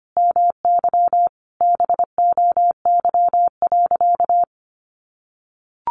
Morse code, heard as a keyed tone, about 25 wpm, 700 Hz. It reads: MY BOY.
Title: Sentences from Top 1000 Words 25wpm